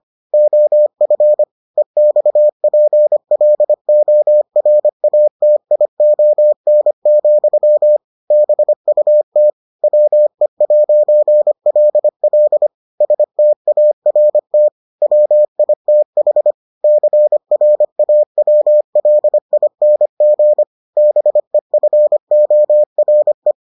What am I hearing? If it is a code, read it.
Morse code, heard as a keyed tone, about 25 wpm, 600 Hz. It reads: OF EXPLORATION, BUT WE'LL START WITH CRAWLING BEFORE